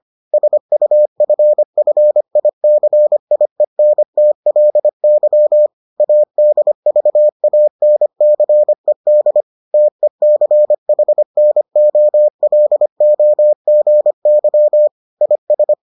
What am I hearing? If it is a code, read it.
Morse code, heard as a keyed tone, about 25 wpm, 600 Hz. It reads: SUFFICIENTLY ADVANCED TECHNOLOGY IS